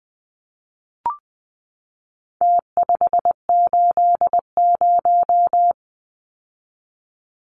Morse code, heard as a keyed tone, about 20 wpm, 700 Hz. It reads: T580
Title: Numbers - Contest Style - 4-Digit 20wpm